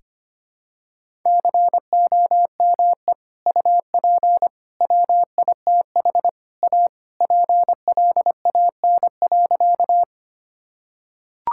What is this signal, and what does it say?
Morse code, heard as a keyed tone, about 25 wpm, 700 Hz. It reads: COME UP WITH A PLAN.
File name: practice / 25wpm / Sentences with 3-Word Phrases - 25wpm